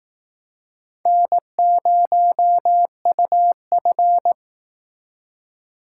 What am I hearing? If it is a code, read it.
Morse code, heard as a keyed tone, about 18 wpm, 700 Hz. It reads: N0UF
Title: Speed Racing - USA 1x2 Call Signs 15wpm